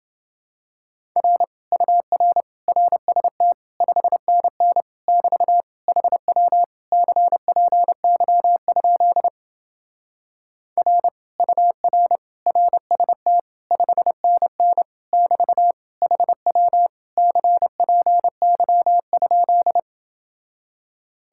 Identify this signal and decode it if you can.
Morse code, heard as a keyed tone, about 29 wpm, 700 Hz. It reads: R UR RST 5NN <BT> HW CPY? R UR RST 5NN <BT> HW CPY?